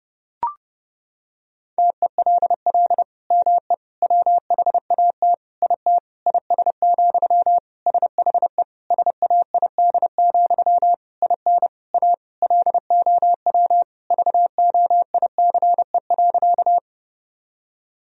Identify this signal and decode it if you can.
Morse code, heard as a keyed tone, about 32 wpm, 700 Hz. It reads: TELL ME WHAT IT IS, SHE SAID, IN A LOW VOICE.
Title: Sentences from Top 1000 Words - Review Part 1 - 30wpm